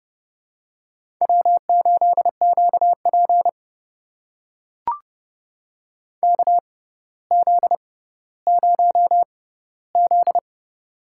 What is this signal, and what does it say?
Morse code, heard as a keyed tone, about 30 wpm, spaced as for 15 wpm, 700 Hz. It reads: W8QP KZ0Z
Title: USA Call Signs - 4 Characters - 6x Character Spacing - 30wpm